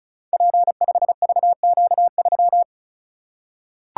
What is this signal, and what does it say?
Morse code, heard as a keyed tone, about 36 wpm, 700 Hz. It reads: P5VQ3
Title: Code Groups - 5 Letter:Numbers - 35wpm